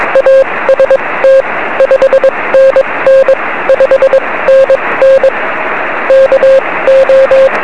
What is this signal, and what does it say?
Morse code, heard as a keyed tone, about 22 wpm, spaced as for 18 wpm, 530 Hz, with radio noise. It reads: AST 5NN 5NN KO
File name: 1823z yl2aj 23cm